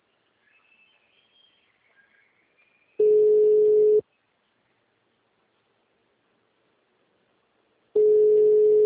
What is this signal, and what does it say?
Ringback tone (Europe, 425 Hz).
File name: out- -2004- - - .23